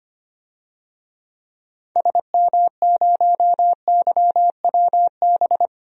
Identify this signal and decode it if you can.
Morse code, heard as a keyed tone, about 25 wpm, 700 Hz. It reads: SM0YWB
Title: DXCC-100 Call Signs - 6 Characters - 25wpm